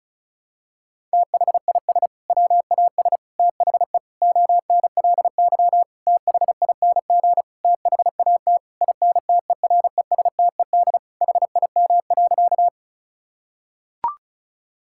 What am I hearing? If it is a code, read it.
Morse code, heard as a keyed tone, about 36 wpm, 700 Hz. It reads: THIS WAS THE ONLY THING THAT INTERESTED HIM.
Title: Sentences from Top 1000 Words - Review Part 6 - 35wpm